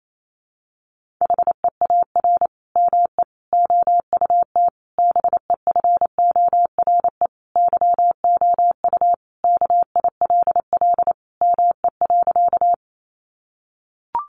Morse code, heard as a keyed tone, about 28 wpm, 700 Hz. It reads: HEAR ME OUT BEFORE YOU KILL ME.
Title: Sentences from Top 1400 Words 28wpm